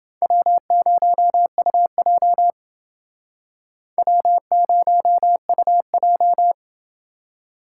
Morse code, heard as a keyed tone, about 29 wpm, 700 Hz. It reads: W0UJ W0UJ